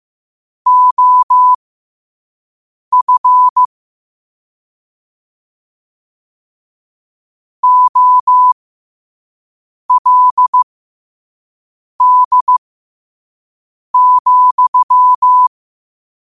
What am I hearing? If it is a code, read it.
Morse code, heard as a keyed tone, about 15 wpm, spaced as for 5 wpm, 1000 Hz. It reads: OF OLD,
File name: qotdfiles / 5wpm